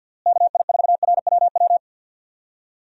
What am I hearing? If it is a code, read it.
Morse code, heard as a keyed tone, about 51 wpm, 700 Hz. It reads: KI4RWW
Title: USA Call Signs - 6 Characters - Repeated 2x - 50wpm